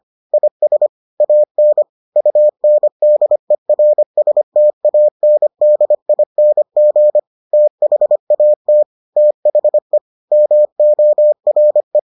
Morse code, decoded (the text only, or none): IS AN UNDERSTANDING THAT THE MORE